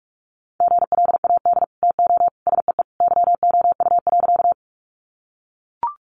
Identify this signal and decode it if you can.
Morse code, heard as a keyed tone, about 44 wpm, 700 Hz. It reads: GLAD TO SEE YOU.
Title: Sentences from Top 800 Words 45wpm